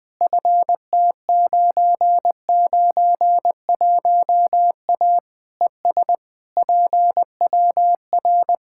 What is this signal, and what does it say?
Morse code, heard as a keyed tone, about 20 wpm, 700 Hz. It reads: FT991A ES PWR
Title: K1USN SST QSO Practice 20wpm